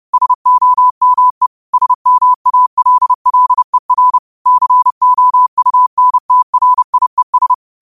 Morse code, heard as a keyed tone, about 30 wpm, 1000 Hz. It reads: SOME SMALLER COUNTRIES